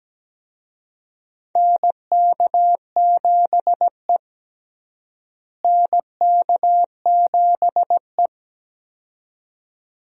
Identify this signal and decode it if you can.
Morse code, heard as a keyed tone, about 17 wpm, 700 Hz. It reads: NK7E NK7E
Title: USA Call Signs - 4 Characters - Repeated 2x - 17wpm